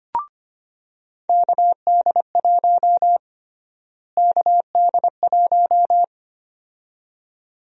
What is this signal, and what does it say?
Morse code, heard as a keyed tone, about 25 wpm, 700 Hz. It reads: KD1 KD1
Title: USA Call Sign Prefixes - Repeated 2x - 25wpm